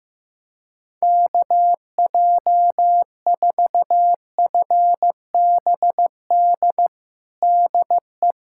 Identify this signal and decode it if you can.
Morse code, heard as a keyed tone, about 15 wpm, 700 Hz. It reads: KJ4FBD DE
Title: USA Callsign DE Callsign - 6 Characters - 15wpm